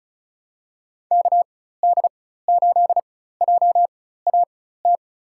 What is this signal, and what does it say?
Morse code, heard as a keyed tone, about 36 wpm, spaced as for 16 wpm, 700 Hz. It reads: KD8JAT
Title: USA Call Signs - 6 Characters - 4x Character Spacing - 35wpm